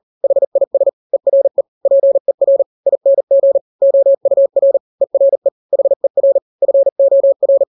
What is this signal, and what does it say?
Morse code, heard as a keyed tone, about 40 wpm, 550 Hz. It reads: HIS ERE PER ING OUR ERE HER FOR
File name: n-grams / 3-grams 40 35 WPM